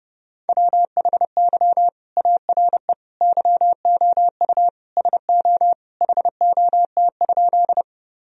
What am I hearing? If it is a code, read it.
Morse code, heard as a keyed tone, about 30 wpm, 700 Hz. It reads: WHY ARE YOU SO HOT?
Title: Sentences from Top 1000 Words - Review Part 3 - 30wpm